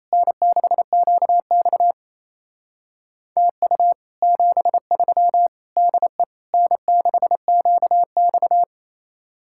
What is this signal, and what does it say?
Morse code, heard as a keyed tone, about 33 wpm, 700 Hz. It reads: N6QX TU 73 DE N6QX
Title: Speed-Racing Common QSO Phrases Encore - 28wpm